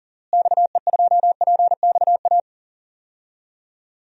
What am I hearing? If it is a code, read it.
Morse code, heard as a keyed tone, about 40 wpm, 700 Hz. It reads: XE2PXA
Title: DXCC-100 Call Signs - 6 Characters - Repeated 3x - 40wpm